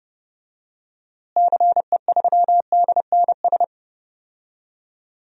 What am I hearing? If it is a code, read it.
Morse code, heard as a keyed tone, about 30 wpm, 700 Hz. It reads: CE3DNS
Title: DXCC-100 Call Signs - 6 Characters - Repeated 2x - 30wpm